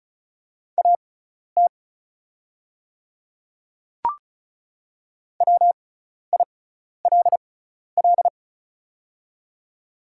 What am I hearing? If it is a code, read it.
Morse code, heard as a keyed tone, about 36 wpm, spaced as for 12 wpm, 690 Hz. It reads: AT WILL
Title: 6x Character Spacing - Top 200 Words - 35wpm